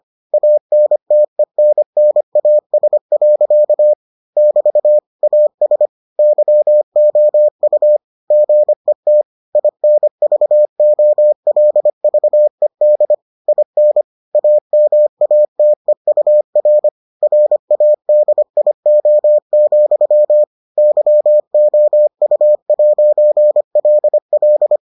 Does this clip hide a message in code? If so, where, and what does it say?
Yes: Morse code from 0.3 to 24.8 s, heard as a keyed tone, about 25 wpm, 600 Hz. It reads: ANTENNAS. <BT> AS YOU GET INVOLVED IN AMATEUR RADIO, YOU'LL